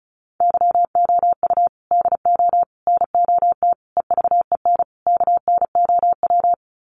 Morse code, heard as a keyed tone, about 34 wpm, 700 Hz. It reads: YOU DO NOT EVEN KNOW
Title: Sets of 5 Words Encore2 35wpm